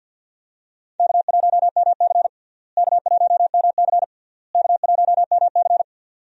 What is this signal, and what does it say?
Morse code, heard as a keyed tone, about 52 wpm, 700 Hz. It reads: K1MC K1MC K1MC
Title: USA Call Signs - 4 Characters - Repeated 3x - 50wpm